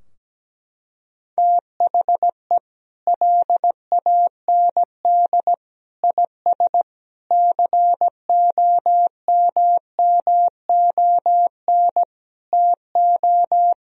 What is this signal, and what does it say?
Morse code, heard as a keyed tone, about 17 wpm, 700 Hz. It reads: THE LAND IS COMMON TO
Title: Sentences from Top 1000 Words - Review Part 3 - 17wpm